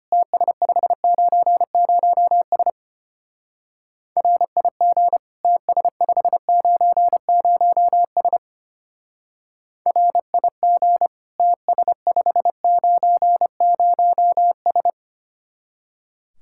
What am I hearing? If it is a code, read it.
Morse code, heard as a keyed tone, about 32 wpm, 700 Hz. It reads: TS590S RIG TS590S RIG TS590S